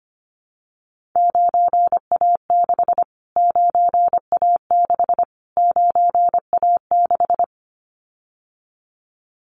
Morse code, heard as a keyed tone, about 25 wpm, 700 Hz. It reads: 9A6 9A6 9A6
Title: Callsign Prefixes for Top 25 DXCC - Repeated 3x - 25wpm